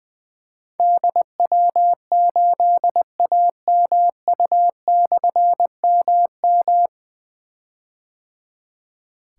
Morse code, heard as a keyed tone, about 20 wpm, 700 Hz. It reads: DW8AMU/MM
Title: Difficult Call Signs 20wpm